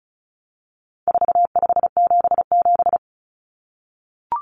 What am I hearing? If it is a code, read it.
Morse code, heard as a keyed tone, about 34 wpm, 700 Hz. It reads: 4577
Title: Common Cut Numbers - 4-Digit 35wpm